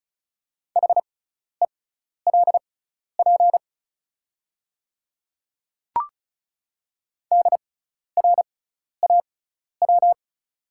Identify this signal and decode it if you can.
Morse code, heard as a keyed tone, about 36 wpm, spaced as for 12 wpm, 690 Hz. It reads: HELP DRAW
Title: 6x Character Spacing - Top 300 Words - 35wpm